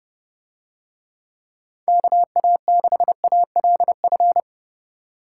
Morse code, heard as a keyed tone, about 29 wpm, 700 Hz. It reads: KA6ALF